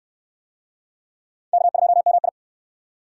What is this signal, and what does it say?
Morse code, heard as a keyed tone, about 64 wpm, 700 Hz. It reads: K2MI